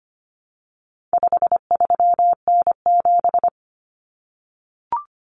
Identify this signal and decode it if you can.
Morse code, heard as a keyed tone, about 25 wpm, 700 Hz. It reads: 53N7
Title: Common Cut Numbers - 4-Digit 25wpm